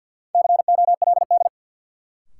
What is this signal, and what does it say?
Morse code, heard as a keyed tone, about 52 wpm, 700 Hz. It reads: COLD
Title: Sentences from Top 400 Words - New Word Drill - 50wpm